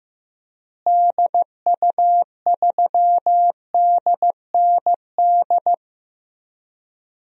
Morse code, heard as a keyed tone, about 15 wpm, 700 Hz. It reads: DU3DND
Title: DXCC-100 Call Signs - 6 Characters - Repeated 2x - 15wpm